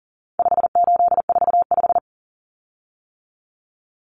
Morse code, heard as a keyed tone, about 40 wpm, 700 Hz. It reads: 5845
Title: Common Cut Numbers - 4-Digit 40wpm